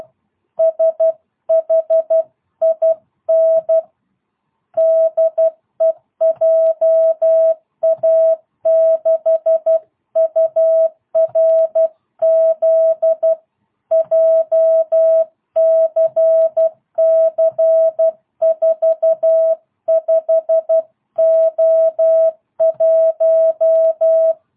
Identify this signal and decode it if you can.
Morse code, heard as a keyed tone, about 12 wpm, 660 Hz. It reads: SHIN DEJA6URZ JCC45O1